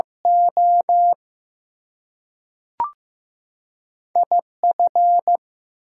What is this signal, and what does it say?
Morse code, heard as a keyed tone, about 15 wpm, 700 Hz. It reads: O IF
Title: Two Letter Words - 15wpm